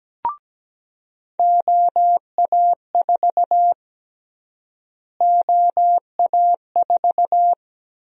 Morse code, heard as a keyed tone, about 17 wpm, 700 Hz. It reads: OA4 OA4
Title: USA Call Sign Prefixes - Repeated 2x - 17wpm